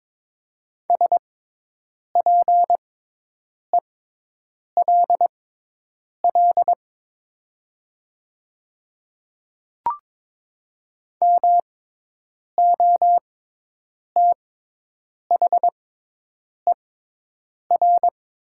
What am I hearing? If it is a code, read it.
Morse code, heard as a keyed tone, about 22 wpm, spaced as for 8 wpm, 700 Hz. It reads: SPELL MOTHER
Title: 6x Character Spacing - Top 200 Words - 22wpm